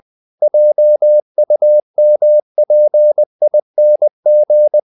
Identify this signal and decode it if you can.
Morse code, heard as a keyed tone, about 20 wpm, 600 Hz. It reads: JUMPING
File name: the old man and the sea part 2 20wpm